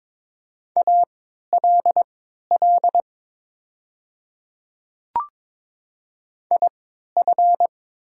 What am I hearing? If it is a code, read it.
Morse code, heard as a keyed tone, about 22 wpm, spaced as for 13 wpm, 700 Hz. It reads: ALL IF
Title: 3x Character Spacing - Top 200 Words - 22wpm